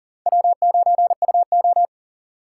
Morse code, heard as a keyed tone, about 50 wpm, 700 Hz. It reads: W9UO
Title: Speed Racing - USA Call Signs 40wpm